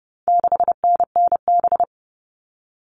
Morse code, heard as a keyed tone, about 30 wpm, 700 Hz. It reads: 6NNB
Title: All Cut Numbers - 4-Digit 30wpm